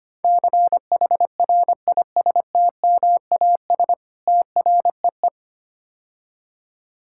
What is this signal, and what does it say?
Morse code, heard as a keyed tone, about 26 wpm, 700 Hz. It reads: CHRISTMAS TREE
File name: practice / 25wpm / Difficult Words - Speed-Racing with Hints - 25wpm